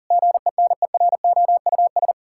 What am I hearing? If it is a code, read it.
Morse code, heard as a keyed tone, about 40 wpm, 700 Hz. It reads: GENEROUS
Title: Pangram Challenge v2 - 40wpm